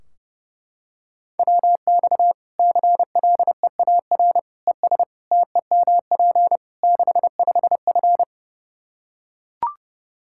Morse code, heard as a keyed tone, about 30 wpm, 700 Hz. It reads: WX CLEAR ES TEMP 65F
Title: Practice QSO - Exchange 2 - Info - 30wpm